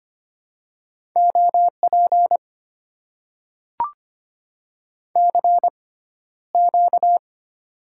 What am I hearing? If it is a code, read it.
Morse code, heard as a keyed tone, about 25 wpm, spaced as for 13 wpm, 700 Hz. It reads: OP CQ
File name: Top 90 QSO Elements - 6x Character Spacing - 25wpm